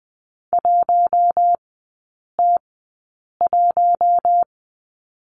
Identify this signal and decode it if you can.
Morse code, heard as a keyed tone, about 20 wpm, spaced as for 8 wpm, 700 Hz. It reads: 1T1
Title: 041 Lesson 8e - Number 1 - Three Characters 20wpm